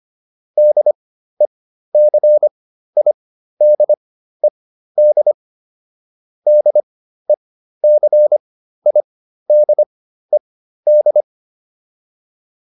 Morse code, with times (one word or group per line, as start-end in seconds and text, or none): 0.6-5.3 DECIDED
6.5-11.2 DECIDED